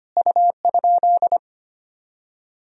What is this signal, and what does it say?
Morse code, heard as a keyed tone, about 25 wpm, 700 Hz. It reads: U?
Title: Sentences from Top 100 Words and Abbreviations 25wpm